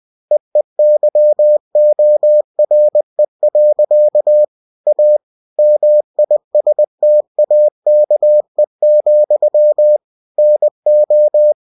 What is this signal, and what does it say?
Morse code, heard as a keyed tone, about 20 wpm, 600 Hz. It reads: EEYORE. A MISTAKE, NO
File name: cw-books / winnie-the-pooh / winnie-20wpm-page-0105